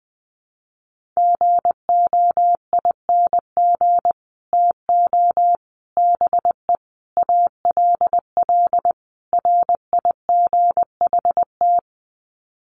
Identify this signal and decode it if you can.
Morse code, heard as a keyed tone, about 20 wpm, 700 Hz. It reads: GOING TO BE ALL RIGHT
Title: Sets of 5 Words Encore2 20wpm